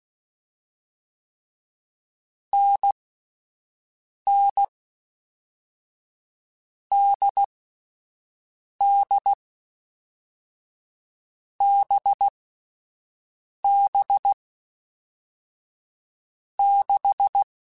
Morse code, heard as a keyed tone, about 16 wpm, 800 Hz. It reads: N N D D B B 6
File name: Short Runs / Short Runs-mp3s / awj1ndb6gz7 @10